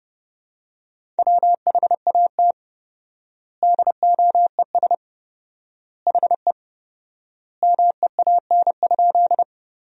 Morse code, heard as a keyed tone, about 32 wpm, 700 Hz. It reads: WHAT DOES HE MEAN?
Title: 4x Word-Spacing - Sentences from Top 200 Words - 30wpm